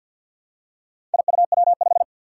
Morse code, heard as a keyed tone, about 52 wpm, 700 Hz. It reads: IUW5
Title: Four Challenging Letter-Number Combinations 50wpm